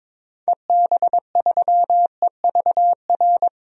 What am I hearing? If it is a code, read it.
Morse code, heard as a keyed tone, about 22 wpm, 700 Hz. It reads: EB3EVR
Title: DXCC-100 Call Signs - 6 Characters - Repeated 3x - 22wpm